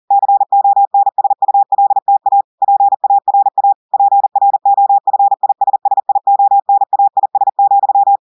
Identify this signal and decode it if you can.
Morse code, heard as a keyed tone, about 40 wpm, 800 Hz. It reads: CONSULTA PARA PROFISSIONAIS,
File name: audios / portugais / portugais texte en clair 1 vitesse 40 mots minutes